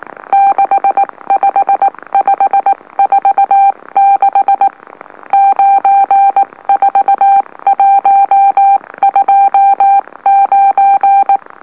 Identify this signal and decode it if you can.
Morse code, heard as a keyed tone, about 19 wpm, spaced as for 16 wpm, 790 Hz. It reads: 65546 94129
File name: M04- -2345utc-LOLO